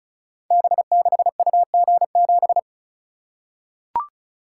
Morse code, heard as a keyed tone, about 36 wpm, 700 Hz. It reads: B6UG7
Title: Code Groups - 5 Letter:Numbers - 35wpm